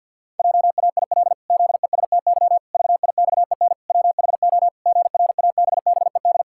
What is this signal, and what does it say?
Morse code, heard as a keyed tone, about 50 wpm, 700 Hz. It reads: JAIL ZESTY VIXEN WHO GRABBED
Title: Pangram Challenge v2 - 50wpm